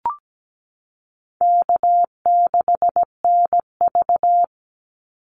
Morse code, heard as a keyed tone, about 17 wpm, 700 Hz. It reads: K6NV
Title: Easy Call Signs Repeated 2x - 17wpm